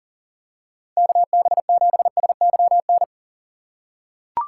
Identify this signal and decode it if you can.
Morse code, heard as a keyed tone, about 40 wpm, 700 Hz. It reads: KB7SYN
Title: USA Call Signs - 6 Characters - 6x Character Spacing - 40wpm